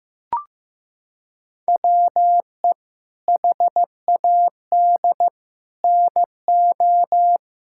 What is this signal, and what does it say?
Morse code, heard as a keyed tone, about 15 wpm, 700 Hz. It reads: WE HAD NO
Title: Sentences from Top 1000 Words - Review Part 6 - 15wpm